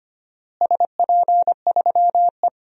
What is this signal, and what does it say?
Morse code, heard as a keyed tone, about 25 wpm, 700 Hz. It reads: SP3E
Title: DXCC-100 Call Signs - 4 Characters - Repeated 3x - 25wpm